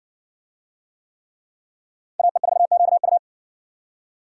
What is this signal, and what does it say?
Morse code, heard as a keyed tone, about 60 wpm, 700 Hz. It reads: AE4QU